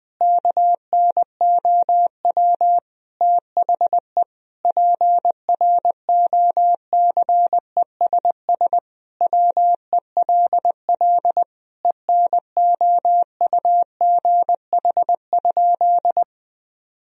Morse code, heard as a keyed tone, about 20 wpm, 700 Hz. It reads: KNOW THE PROCESS WELL ENOUGH?
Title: Sentences from Top 1000 Words - Review Part 3 - 20wpm